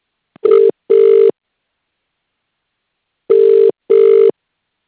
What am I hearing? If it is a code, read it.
Ringback tone (UK), 400 + 450 Hz.